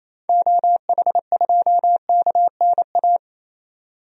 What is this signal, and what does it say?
Morse code, heard as a keyed tone, about 28 wpm, 700 Hz. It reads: OH2KNA
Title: DXCC-100 Call Signs - 6 Characters - 28wpm